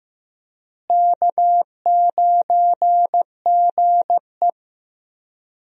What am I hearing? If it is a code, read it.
Morse code, heard as a keyed tone, about 15 wpm, 700 Hz. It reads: K9GE
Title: USA Call Signs - 4 Characters - Repeated 3x - 15wpm